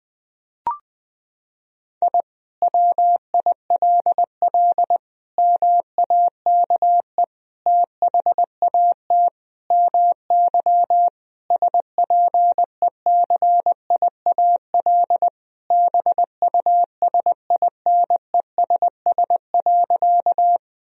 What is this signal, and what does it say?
Morse code, heard as a keyed tone, about 21 wpm, 700 Hz. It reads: I WILL MAKE THAT MY SPECIAL BUSINESS.
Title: Sentences from Top 1000 Words - Review Part 3 - 20wpm